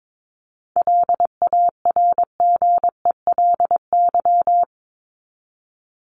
Morse code, heard as a keyed tone, about 22 wpm, 700 Hz. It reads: LARGELY
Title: LY Suffix - Top 100 Words with Base Words 22wpm